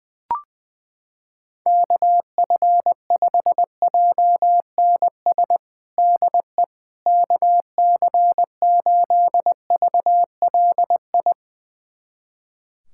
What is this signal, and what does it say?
Morse code, heard as a keyed tone, about 20 wpm, 700 Hz. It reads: KF5JNS DE KC8VLI
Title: USA Callsign DE Callsign - 6 Characters - 20wpm